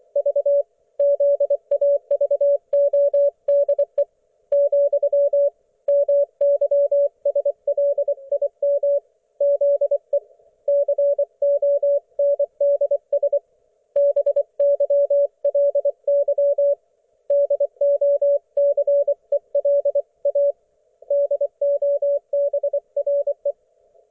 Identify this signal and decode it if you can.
Morse code, heard as a keyed tone, about 24 wpm, 570 Hz. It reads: V ZAVODE , MYSLIM ZE CONDS BYLY DOCELA DOBRE
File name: Archiv / CUC224 / ok1fhi cul224